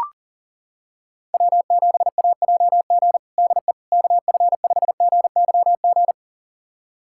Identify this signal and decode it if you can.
Morse code, heard as a keyed tone, about 40 wpm, 700 Hz. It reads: W7AJG DE KF5GYG
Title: USA Callsign DE Callsign 40wpm